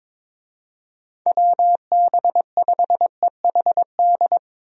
Morse code, heard as a keyed tone, about 22 wpm, 700 Hz. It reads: WB5EHD
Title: USA Call Signs - 6 Characters - Repeated 2x - 22wpm